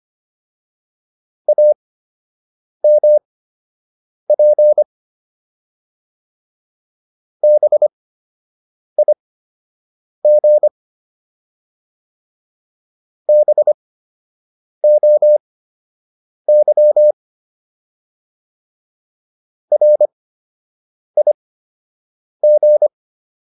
Morse code, heard as a keyed tone, about 25 wpm, spaced as for 7 wpm, 600 Hz. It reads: AMP BIG BOY RIG